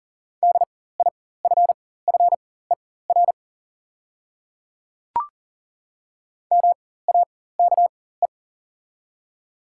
Morse code, heard as a keyed tone, about 40 wpm, spaced as for 19 wpm, 700 Hz. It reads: DIFFER MAKE
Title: 4x Character Spacing - Top 200 Words - 40wpm